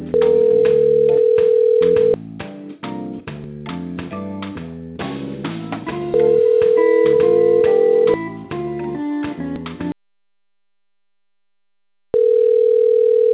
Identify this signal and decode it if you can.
Ringback tone (North America), 440 + 480 Hz.